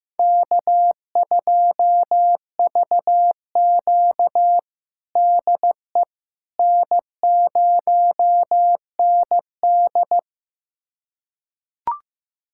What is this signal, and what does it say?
Morse code, heard as a keyed tone, about 16 wpm, 700 Hz. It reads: K2VQ DE N0ND
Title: USA Callsign DE Callsign - 4 Characters - 15wpm